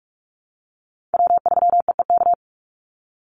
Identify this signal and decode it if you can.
Morse code, heard as a keyed tone, about 44 wpm, 700 Hz. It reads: W3EEK
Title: Call Signs - 45wpm